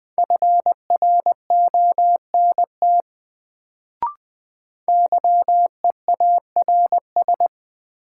Morse code, heard as a keyed tone, about 20 wpm, 700 Hz. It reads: FRONT YEARS
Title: Sets of 2 Words - New Word Drill - 20wpm